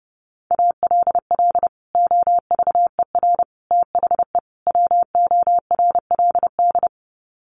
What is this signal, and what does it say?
Morse code, heard as a keyed tone, about 30 wpm, 700 Hz. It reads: ALL OVER THE WORLD
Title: Sets of 4 Words 30wpm